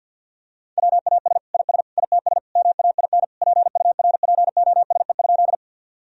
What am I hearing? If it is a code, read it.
Morse code, heard as a keyed tone, about 47 wpm, 700 Hz. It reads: WAS IS ITS MAIN PURPOSE?
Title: Sentences from Top 1000 Words - Review Part 2 - 50wpm